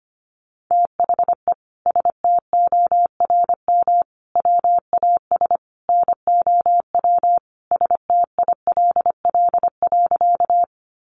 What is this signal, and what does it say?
Morse code, heard as a keyed tone, about 25 wpm, 700 Hz. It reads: THE STORM WAS NOW STILL.
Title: Sentences from Top 1500 Words 25wpm